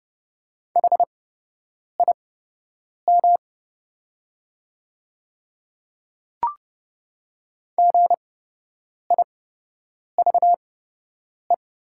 Morse code, heard as a keyed tone, about 30 wpm, spaced as for 8 wpm, 700 Hz. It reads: HIM GIVE
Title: 8x Character Spacing - Top 300 Words - 30wpm